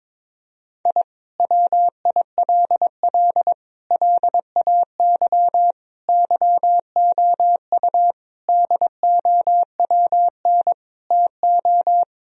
Morse code, heard as a keyed tone, about 22 wpm, 700 Hz. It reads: I WILL LAY YOU DOWN TO
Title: Sentences from Top 1000 Words - Review Part 2 - 22wpm